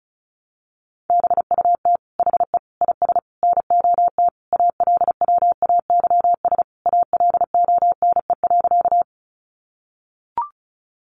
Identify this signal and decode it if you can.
Morse code, heard as a keyed tone, about 34 wpm, 700 Hz. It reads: BUT HE IS NOT ALWAYS ALONE.
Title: Sentences from Top 1000 Words 35wpm